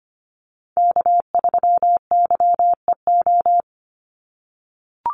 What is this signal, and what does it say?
Morse code, heard as a keyed tone, about 25 wpm, 700 Hz. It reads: K3YEO
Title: Call Signs - 25wpm